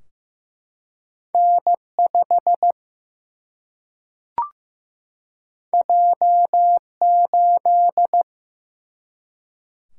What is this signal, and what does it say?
Morse code, heard as a keyed tone, about 15 wpm, 700 Hz. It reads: N5 J8
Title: Single-Letter Single-Number 15wpm